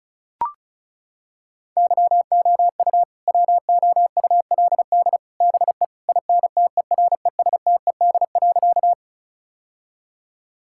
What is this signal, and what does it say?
Morse code, heard as a keyed tone, about 36 wpm, 700 Hz. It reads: YOU WOULD BE INTERESTED.
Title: Sentences from Top 1000 Words - Review Part 6 - 35wpm